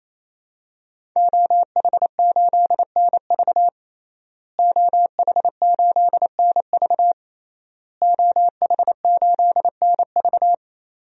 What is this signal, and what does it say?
Morse code, heard as a keyed tone, about 28 wpm, 700 Hz. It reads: OH8NV OH8NV OH8NV
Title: DXCC-100 Call Signs - 5 Characters - Repeated 3x - 28wpm